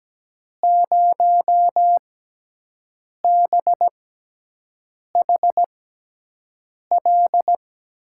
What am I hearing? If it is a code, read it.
Morse code, heard as a keyed tone, about 17 wpm, spaced as for 6 wpm, 700 Hz. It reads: 0BHL
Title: USA Call Signs - 6x Character Spacing - 17wpm